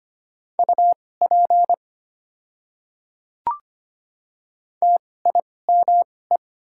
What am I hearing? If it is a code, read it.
Morse code, heard as a keyed tone, about 25 wpm, spaced as for 18 wpm, 700 Hz. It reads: UP TIME
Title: 2x Character Spacing - Top 200 Words - 25wpm